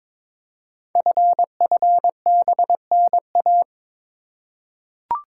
Morse code, heard as a keyed tone, about 22 wpm, 700 Hz. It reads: FFBNA
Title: Code Groups - 5 Letters - 22wpm